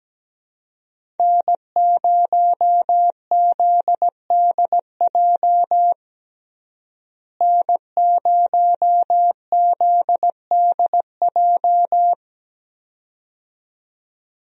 Morse code, heard as a keyed tone, about 17 wpm, 700 Hz. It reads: N0ZDJ N0ZDJ
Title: USA Call Signs - 5 Characters - Repeated 2x - 17wpm